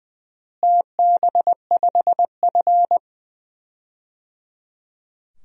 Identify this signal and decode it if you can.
Morse code, heard as a keyed tone, about 21 wpm, 700 Hz. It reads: TB5F
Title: DXCC-100 Call Signs - 4 Characters - Repeated 3x - 20wpm